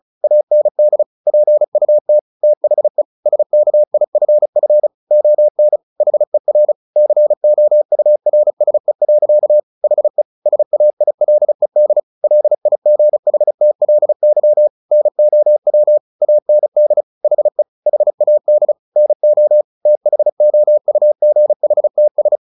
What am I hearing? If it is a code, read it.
Morse code, heard as a keyed tone, about 34 wpm, 600 Hz. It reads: AND PUT THE SKIFF ON HER COURSE. HE SAILED LIGHTLY NOW AND HE HAD NO THOUGHTS
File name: the old man and the sea 2 35wpm